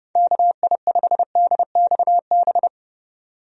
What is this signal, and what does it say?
Morse code, heard as a keyed tone, about 30 wpm, 700 Hz. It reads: KI5DXB
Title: USA Call Signs - 6 Characters - 30wpm